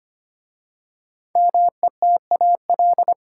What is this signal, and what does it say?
Morse code, heard as a keyed tone, about 25 wpm, 700 Hz. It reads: METAL
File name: 5 Letter Words - First 3 Letters Given - 25wpm